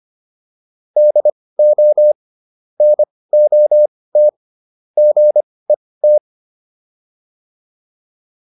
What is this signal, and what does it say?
Morse code, heard as a keyed tone, about 25 wpm, spaced as for 18 wpm, 600 Hz. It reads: DO NOT GET